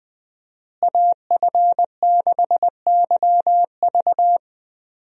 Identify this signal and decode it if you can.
Morse code, heard as a keyed tone, about 20 wpm, 700 Hz. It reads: AF6YV
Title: USA Call Signs - 5 Characters - Repeated 2x - 20wpm